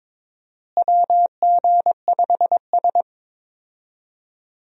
Morse code, heard as a keyed tone, about 22 wpm, 700 Hz. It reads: WG5S
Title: USA Call Signs - 4 Characters - Repeated 2x - 22wpm